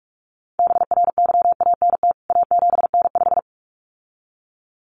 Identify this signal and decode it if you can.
Morse code, heard as a keyed tone, about 44 wpm, 700 Hz. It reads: BRYANT A7N5
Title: CWT Exchange 45wpm